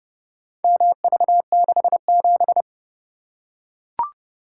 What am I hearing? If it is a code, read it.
Morse code, heard as a keyed tone, about 30 wpm, 700 Hz. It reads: MV67